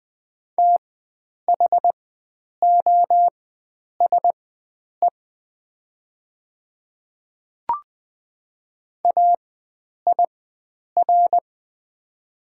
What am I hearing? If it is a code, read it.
Morse code, heard as a keyed tone, about 20 wpm, spaced as for 9 wpm, 700 Hz. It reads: THOSE AIR
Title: 4x Character Spacing - Top 500 Words - 20wpm